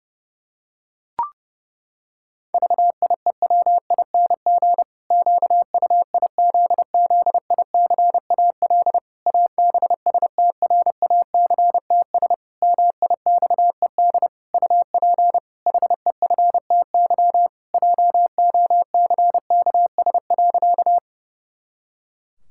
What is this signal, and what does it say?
Morse code, heard as a keyed tone, about 32 wpm, 700 Hz. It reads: VIEWING QUIZZICAL ABSTRACTS MIXED UP HEFTY JOCKS.
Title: Pangram Challenge v2 - 30wpm